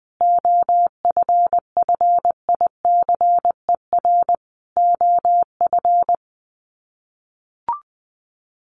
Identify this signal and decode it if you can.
Morse code, heard as a keyed tone, about 19 wpm, 700 Hz. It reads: OFFICER OF
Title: Sets of 5 Words Encore2 20wpm